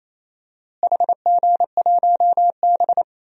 Morse code, heard as a keyed tone, about 28 wpm, 700 Hz. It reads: HG1B
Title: DXCC-100 Call Signs - 4 Characters - Repeated 3x - 28wpm